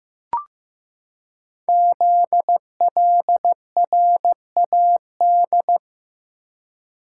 Morse code, heard as a keyed tone, about 15 wpm, 700 Hz. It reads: ZLRAD
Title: Code Groups - 5 Letter:Numbers - 15wpm